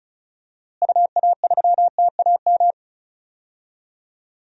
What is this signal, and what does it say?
Morse code, heard as a keyed tone, about 36 wpm, 700 Hz. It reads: UA3TAM
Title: DXCC-100 Call Signs - 6 Characters - Repeated 2x - 35wpm